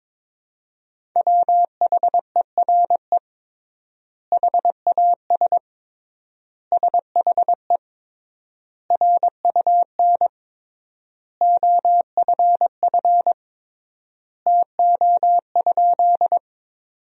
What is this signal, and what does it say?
Morse code, heard as a keyed tone, about 22 wpm, 700 Hz. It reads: WHERE HAS SHE RUN OFF TO?
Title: 3x Word-Spacing - Sentences from Top 300 Words - 22wpm